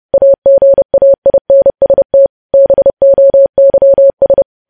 Morse code, heard as a keyed tone, about 30 wpm, 550 Hz. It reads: AGAINST BOYS
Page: News Headlines in Morse Code at 30 WPM